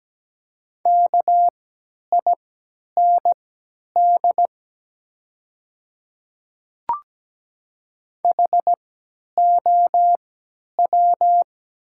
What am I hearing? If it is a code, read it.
Morse code, heard as a keyed tone, about 17 wpm, spaced as for 10 wpm, 700 Hz. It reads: KIND HOW
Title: 3x Character Spacing - Top 200 Words - 17wpm